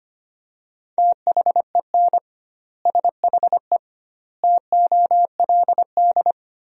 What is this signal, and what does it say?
Morse code, heard as a keyed tone, about 25 wpm, 700 Hz. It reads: THEN SHE TOLD
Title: 2x Word-Spacing - Sentences from Top 400 Words - 25wpm